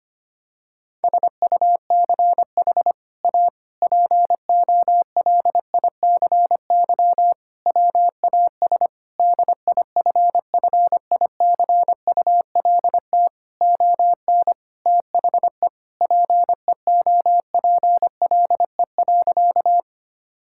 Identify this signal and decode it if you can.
Morse code, heard as a keyed tone, about 25 wpm, 700 Hz. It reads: SUCH A POLICY WAS DIFFICULT ON THE PEOPLE.
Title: Sentences from Top 1000 Words - Review Part 5 - 25wpm